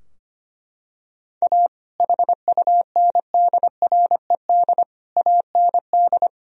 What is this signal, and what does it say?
Morse code, heard as a keyed tone, about 25 wpm, 700 Hz. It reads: A HUNDRED AND
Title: Sentences from Top 1000 Words - Review Part 2 - 25wpm